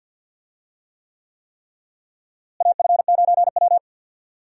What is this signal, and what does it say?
Morse code, heard as a keyed tone, about 51 wpm, 700 Hz. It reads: AF9W
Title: USA Call Signs - 4 Characters - Repeated 3x - 50wpm